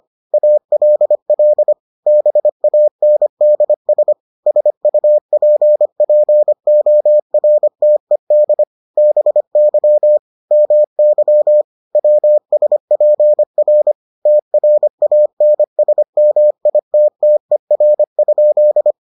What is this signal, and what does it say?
Morse code, heard as a keyed tone, about 25 wpm, 600 Hz. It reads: ALL BANDS SUPPORTED BY MY WSPR TRANSMITTER?